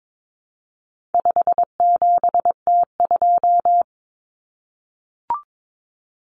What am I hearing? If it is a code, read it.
Morse code, heard as a keyed tone, about 22 wpm, 700 Hz. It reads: 57T2
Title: Common Cut Numbers - 4-Digit 22wpm